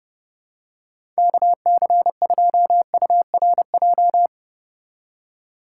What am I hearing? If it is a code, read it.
Morse code, heard as a keyed tone, about 30 wpm, 700 Hz. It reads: KC2URJ
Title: USA Call Signs - 6x Character Spacing - 30wpm